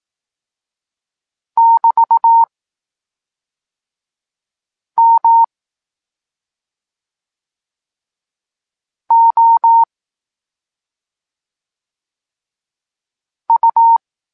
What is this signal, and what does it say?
Morse code, heard as a keyed tone, about 18 wpm, spaced as for 2 wpm, 920 Hz. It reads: <BT> MOU
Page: Tekst 8 – doskonalenie odbioru poznanych liter